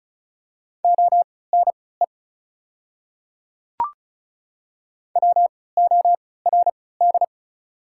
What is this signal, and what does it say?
Morse code, heard as a keyed tone, about 34 wpm, spaced as for 20 wpm, 690 Hz. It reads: ONE WORD
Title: 3x Character Spacing - Top 100 Words - 35wpm